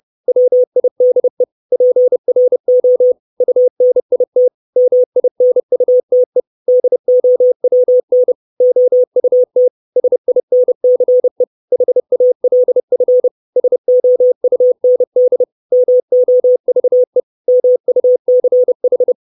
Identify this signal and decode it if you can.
Morse code, heard as a keyed tone, about 30 wpm, 500 Hz. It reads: WIDE PRO UNIT MINUTE DOWN OUT SINCE HALF SOUND MOVE MUCH
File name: 632 Words 30WPM